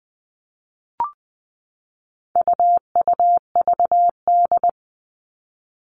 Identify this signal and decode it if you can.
Morse code, heard as a keyed tone, about 20 wpm, 700 Hz. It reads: UUVD
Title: All Cut Numbers - 4-Digit 20wpm